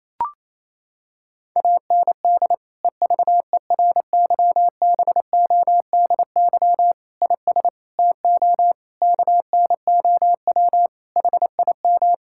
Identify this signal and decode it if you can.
Morse code, heard as a keyed tone, about 28 wpm, 700 Hz. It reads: AND EVERYBODY IS TO KNOW HIM
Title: Sentences from Top 1600 Words 28wpm